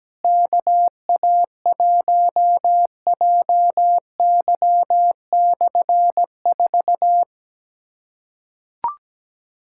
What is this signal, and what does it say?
Morse code, heard as a keyed tone, about 17 wpm, 700 Hz. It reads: KA1JY/4
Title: K1USN SST QSO Practice - 17wpm